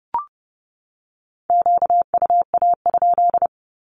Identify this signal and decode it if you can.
Morse code, heard as a keyed tone, about 30 wpm, 700 Hz. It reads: QUA?
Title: Q-Codes 30wpm